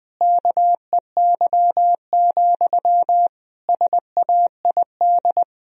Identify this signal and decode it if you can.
Morse code, heard as a keyed tone, about 21 wpm, 700 Hz. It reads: KEY, SAID
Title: Sentences from Top 1000 Words - Review Part 4 - 20wpm